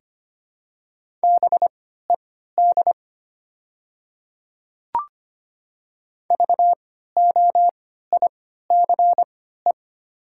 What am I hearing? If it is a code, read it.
Morse code, heard as a keyed tone, about 25 wpm, spaced as for 14 wpm, 700 Hz. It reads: BED VOICE
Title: 3x Character Spacing - Top 500 Words - 25wpm